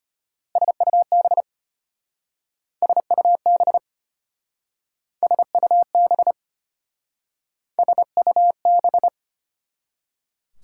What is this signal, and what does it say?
Morse code, heard as a keyed tone, about 32 wpm, 700 Hz. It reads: SUB SUB SUB SUB